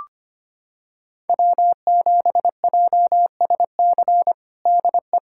Morse code, heard as a keyed tone, about 25 wpm, 700 Hz. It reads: W7JSC DE
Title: USA Callsign DE Callsign - 5 Characters - 25wpm